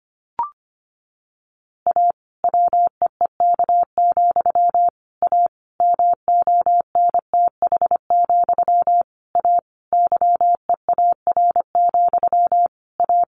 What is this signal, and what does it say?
Morse code, heard as a keyed tone, about 25 wpm, 700 Hz. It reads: A WEEK, A MONTH, A YEAR, A
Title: A Sound Of Thunder 25wpm